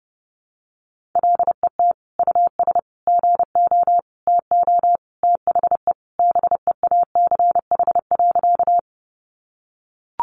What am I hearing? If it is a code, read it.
Morse code, heard as a keyed tone, about 30 wpm, 700 Hz. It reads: LET US GO TO THE BEACH.
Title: Sentences from Top 1500 Words 30wpm